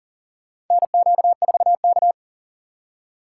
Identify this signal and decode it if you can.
Morse code, heard as a keyed tone, about 40 wpm, 700 Hz. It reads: NQ4K
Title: USA Call Signs - 4 Characters - 4x Character Spacing - 40wpm